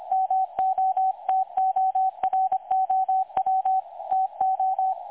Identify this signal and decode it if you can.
Morse code, heard as a keyed tone, about 25 wpm, 760 Hz, with radio noise. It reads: MOTOROW TO
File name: dlazaawansowanych / nagraniazpasma / websdr recording 3565.0kHz